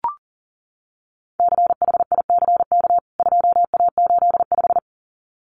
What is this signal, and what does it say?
Morse code, heard as a keyed tone, about 40 wpm, 700 Hz. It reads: CHICK 2A85
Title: CWT Exchange 40wpm